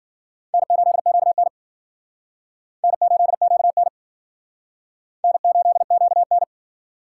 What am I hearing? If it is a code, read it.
Morse code, heard as a keyed tone, about 55 wpm, 700 Hz. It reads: N8QN N8QN N8QN